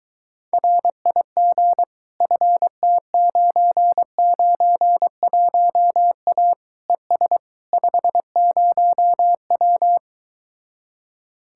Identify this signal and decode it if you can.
Morse code, heard as a keyed tone, about 24 wpm, 700 Hz. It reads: RIG FT991A ES 50W